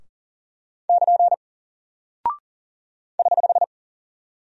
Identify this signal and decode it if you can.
Morse code, heard as a keyed tone, about 40 wpm, 700 Hz. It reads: <KN> <HH>
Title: Single Character (Letter:Number:Punctuation:Prosign) Alt 40wpm